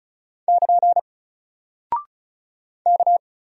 Morse code, heard as a keyed tone, about 36 wpm, 700 Hz. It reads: <KN> K
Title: Single Character (Letter:Number:Punctuation:Prosign) Alt 35wpm